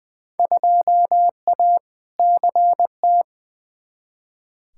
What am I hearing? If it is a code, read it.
Morse code, heard as a keyed tone, about 20 wpm, 700 Hz. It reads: 2A CT
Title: Field Day Exchanges - 20wpm